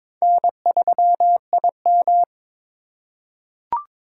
Morse code, heard as a keyed tone, about 21 wpm, 700 Hz. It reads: N3IM
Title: USA Call Signs - 4 Characters - 4x Character Spacing - 22wpm